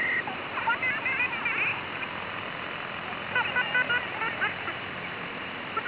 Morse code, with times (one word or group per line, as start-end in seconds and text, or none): none